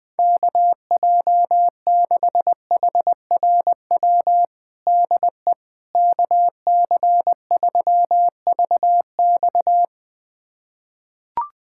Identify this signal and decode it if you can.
Morse code, heard as a keyed tone, about 21 wpm, 700 Hz. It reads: KJ6HRW DE KC3VX
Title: USA Callsign DE Callsign 20wpm